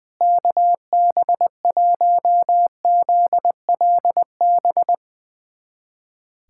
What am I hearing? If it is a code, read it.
Morse code, heard as a keyed tone, about 20 wpm, 690 Hz. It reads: KB1ZLB
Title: USA Callsign DE Callsign 20wpm